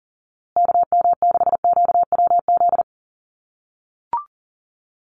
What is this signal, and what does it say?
Morse code, heard as a keyed tone, about 40 wpm, 700 Hz. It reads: KM6QWZ
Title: USA Call Signs Repeated 2x 40wpm